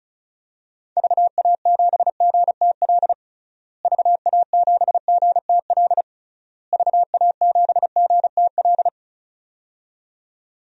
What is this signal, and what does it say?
Morse code, heard as a keyed tone, about 36 wpm, 700 Hz. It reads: VA7GTL VA7GTL VA7GTL
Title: DXCC-100 Call Signs - 6 Characters - Repeated 3x - 35wpm